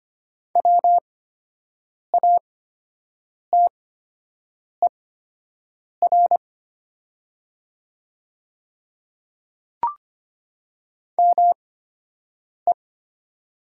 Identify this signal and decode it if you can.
Morse code, heard as a keyed tone, about 24 wpm, spaced as for 7 wpm, 690 Hz. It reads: WATER ME
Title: 8x Character Spacing - Top 200 Words - 25wpm